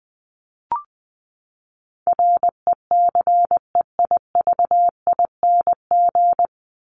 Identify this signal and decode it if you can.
Morse code, heard as a keyed tone, about 20 wpm, 700 Hz. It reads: RECEIVING
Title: ING Suffix - Top 100 Words with Base Words 20wpm